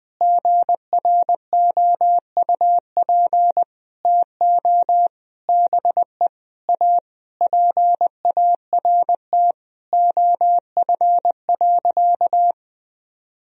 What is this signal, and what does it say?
Morse code, heard as a keyed tone, about 21 wpm, 700 Hz. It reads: GROUP TO BE A PART OF.
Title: Sentences from Top 1000 Words - Review Part 1 - 20wpm